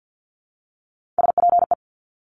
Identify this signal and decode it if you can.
Morse code, heard as a keyed tone, about 50 wpm, 700 Hz. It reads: SPE
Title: Top Three Letter Combinations in Words 50wpm